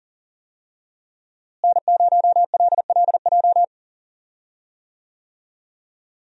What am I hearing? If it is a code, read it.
Morse code, heard as a keyed tone, about 40 wpm, 700 Hz. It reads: N0LLJ
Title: USA Call Signs - 5 Characters - Repeated 2x - 40wpm